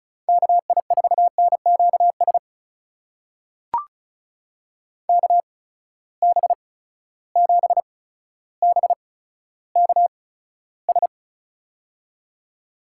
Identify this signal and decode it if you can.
Morse code, heard as a keyed tone, about 36 wpm, spaced as for 15 wpm, 700 Hz. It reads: KI4NQS KB7BKS
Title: USA Call Signs - 6 Characters - 8x Character Spacing - 35wpm